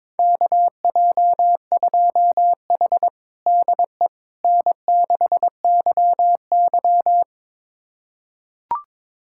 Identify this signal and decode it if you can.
Morse code, heard as a keyed tone, about 22 wpm, 700 Hz. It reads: KJ2H DE N6YY
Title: USA Callsign DE Callsign - 4 Characters - 22wpm